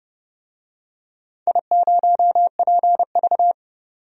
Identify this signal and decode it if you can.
Morse code, heard as a keyed tone, about 30 wpm, 700 Hz. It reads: I0PV
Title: DXCC-100 Call Signs - 4 Characters - 30wpm